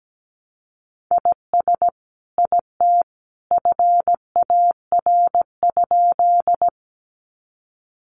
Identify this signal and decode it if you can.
Morse code, heard as a keyed tone, about 17 wpm, 700 Hz. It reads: IS IT FAR?
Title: Sentences from Top 300 Words 17wpm